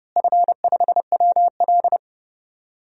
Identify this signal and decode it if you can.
Morse code, heard as a keyed tone, about 30 wpm, 700 Hz. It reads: F5WL
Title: DXCC-100 Call Signs - 4 Characters - Repeated 2x - 30wpm